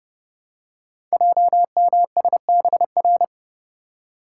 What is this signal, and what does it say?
Morse code, heard as a keyed tone, about 30 wpm, 700 Hz. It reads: JMSBR
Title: Code Groups - 5 Letters - 30wpm